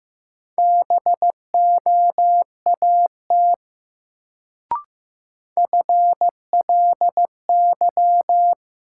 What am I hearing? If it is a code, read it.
Morse code, heard as a keyed tone, about 15 wpm, 700 Hz. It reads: BOAT FLY
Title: Sentences from Top 500 Words - New Word Drill - 15wpm